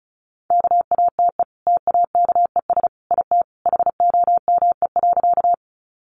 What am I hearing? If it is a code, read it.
Morse code, heard as a keyed tone, about 34 wpm, 700 Hz. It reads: KATE TAKES IT HOME.
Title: Easy Phrases 35wpm